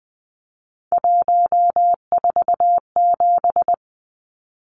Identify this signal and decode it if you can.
Morse code, heard as a keyed tone, about 20 wpm, 700 Hz. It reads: 147
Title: Numbers - 3-Digit 20wpm